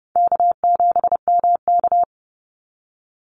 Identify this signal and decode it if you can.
Morse code, heard as a keyed tone, about 30 wpm, 700 Hz. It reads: K7MK
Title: NA SOTA Activator Callsigns 30wpm